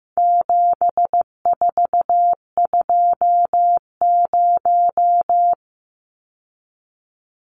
Morse code, heard as a keyed tone, about 15 wpm, 700 Hz. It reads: 7420
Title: Numbers - 4-Digit 15wpm